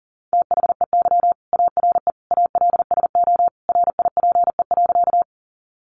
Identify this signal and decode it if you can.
Morse code, heard as a keyed tone, about 40 wpm, 700 Hz. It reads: THEY ARE ALSO RIPE.
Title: Additional Easy Phrases 40wpm